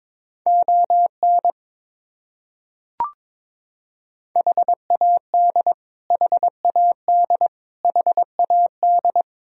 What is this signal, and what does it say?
Morse code, heard as a keyed tone, about 22 wpm, 700 Hz. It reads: ON HAD HAD HAD
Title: Top 100 Words Repeated 22wpm